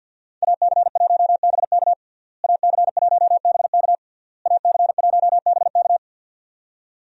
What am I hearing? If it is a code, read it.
Morse code, heard as a keyed tone, about 51 wpm, 700 Hz. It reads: AC1BK AC1BK AC1BK
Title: USA Call Signs - 5 Characters - Repeated 3x - 50wpm